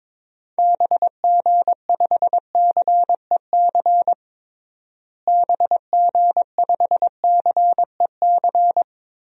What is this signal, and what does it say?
Morse code, heard as a keyed tone, about 22 wpm, 700 Hz. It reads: BG5CEC BG5CEC
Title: DXCC-100 Call Signs - 6 Characters - Repeated 3x - 22wpm